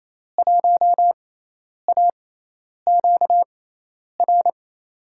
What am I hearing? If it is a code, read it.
Morse code, heard as a keyed tone, about 28 wpm, spaced as for 10 wpm, 700 Hz. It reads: 1AQR
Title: USA Call Signs - 5 Characters - 6x Character Spacing - 28wpm